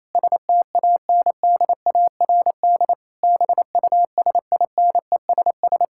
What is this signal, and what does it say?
Morse code, heard as a keyed tone, about 28 wpm, 700 Hz. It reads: STANDARD BUSINESS
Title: Sentences from Top 1000 Words - Review Part 6 - 28wpm